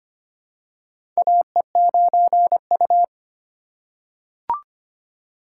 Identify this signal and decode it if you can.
Morse code, heard as a keyed tone, about 25 wpm, 700 Hz. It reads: AE9U
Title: USA Call Signs - 4 Characters - 2x Character Spacing - 25wpm